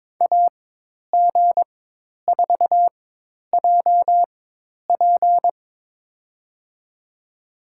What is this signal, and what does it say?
Morse code, heard as a keyed tone, about 21 wpm, spaced as for 10 wpm, 700 Hz. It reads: AG4JP
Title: USA Call Signs - 5 Characters - 4x Character Spacing - 22wpm